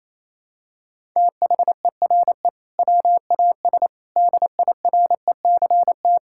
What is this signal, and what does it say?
Morse code, heard as a keyed tone, about 27 wpm, 700 Hz. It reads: THERE WAS DIRECT
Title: Sentences from Top 1000 Words - Review Part 6 - 28wpm